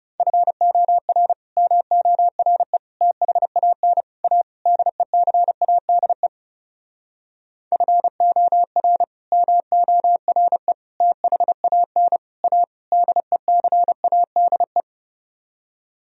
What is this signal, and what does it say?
Morse code, heard as a keyed tone, about 33 wpm, 700 Hz. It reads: FOR MORE THAN A DECADE FOR MORE THAN A DECADE
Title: Speed-Racing - Sets of 5 Words 30wpm